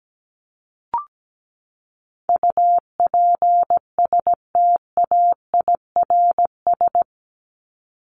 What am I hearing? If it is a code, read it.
Morse code, heard as a keyed tone, about 17 wpm, 700 Hz. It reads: UPSTAIRS
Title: Top 1751-2000 Words 17wpm